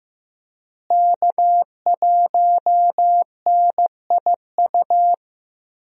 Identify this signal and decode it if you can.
Morse code, heard as a keyed tone, about 16 wpm, 700 Hz. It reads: K1NIU
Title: USA Call Signs - 5 Characters - Repeated 2x - 15wpm